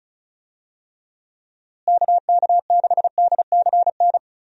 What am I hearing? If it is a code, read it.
Morse code, heard as a keyed tone, about 34 wpm, 700 Hz. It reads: KK6DCN
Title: USA Call Signs - 6 Characters - Repeated 3x - 35wpm